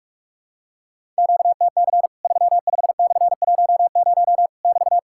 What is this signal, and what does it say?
Morse code, heard as a keyed tone, about 44 wpm, 700 Hz. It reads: QTC 35/10 <BT>
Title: Worked All Europe DX Contest - QTC Exchange - 45wpm